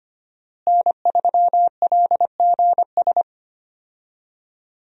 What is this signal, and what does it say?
Morse code, heard as a keyed tone, about 25 wpm, 700 Hz. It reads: N3LGS
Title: USA Call Signs - 5 Characters - Repeated 3x - 25wpm